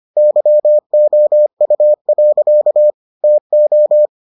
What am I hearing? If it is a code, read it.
Morse code, heard as a keyed tone, about 25 wpm, 600 Hz. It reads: YOU. TO